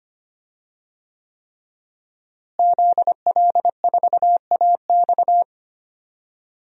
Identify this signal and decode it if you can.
Morse code, heard as a keyed tone, about 25 wpm, 700 Hz. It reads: ZL4AX
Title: DXCC-100 Call Signs - 5 Characters - Repeated 3x - 25wpm